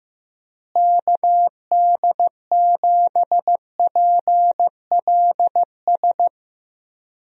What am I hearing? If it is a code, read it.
Morse code, heard as a keyed tone, about 15 wpm, 700 Hz. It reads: KD7PLS
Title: USA Call Signs - 6 Characters - Repeated 2x - 15wpm